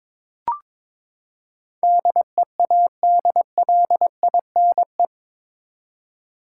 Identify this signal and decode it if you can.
Morse code, heard as a keyed tone, about 22 wpm, 700 Hz. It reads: DEADLINE
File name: practice / 22wpm / Eight Letter Words - 22wpm